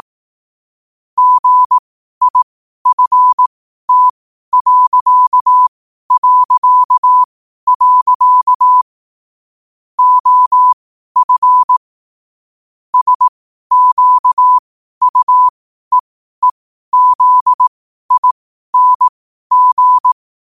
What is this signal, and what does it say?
Morse code, heard as a keyed tone, about 18 wpm, spaced as for 12 wpm, 1000 Hz. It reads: GIFT... OF SQUEEZING
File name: qotdfiles / 12wpm